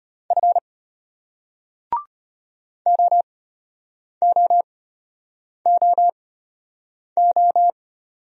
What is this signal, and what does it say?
Morse code, heard as a keyed tone, about 33 wpm, spaced as for 8 wpm, 700 Hz. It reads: F OOOO